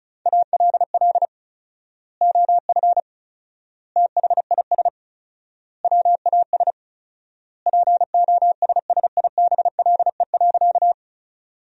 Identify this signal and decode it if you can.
Morse code, heard as a keyed tone, about 36 wpm, 700 Hz. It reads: ALL OF THIS WAS POSSIBLE.
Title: 4x Word-Spacing - Sentences from Top 500 Words - 35wpm